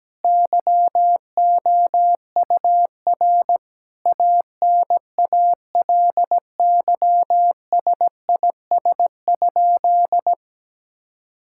Morse code, heard as a keyed tone, about 17 wpm, 700 Hz. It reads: YOUR ANALYSIS?
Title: Sentences from Top 1000 Words - Review Part 5 - 17wpm